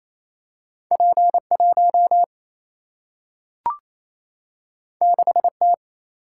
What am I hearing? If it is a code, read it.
Morse code, heard as a keyed tone, about 28 wpm, 700 Hz. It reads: P 1 6 T
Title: Single-Letter Single-Number 28wpm